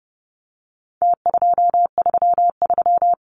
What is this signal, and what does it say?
Morse code, heard as a keyed tone, about 30 wpm, 700 Hz. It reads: T233
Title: Numbers - Contest Style - 4-Digit 30wpm